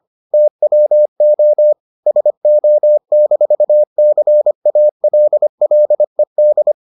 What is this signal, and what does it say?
Morse code, heard as a keyed tone, about 25 wpm, 600 Hz. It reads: TWO SO-CALLED